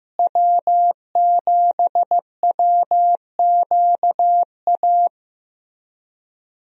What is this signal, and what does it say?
Morse code, heard as a keyed tone, about 15 wpm, 700 Hz. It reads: W7WQA
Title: USA Call Signs - 5 Characters - Repeated 3x - 15wpm